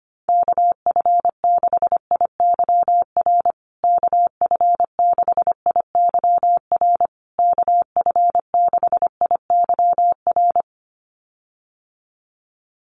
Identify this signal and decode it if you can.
Morse code, heard as a keyed tone, about 25 wpm, 700 Hz. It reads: KF6IYR KF6IYR KF6IYR
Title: USA Call Signs Repeated 3x 25wpm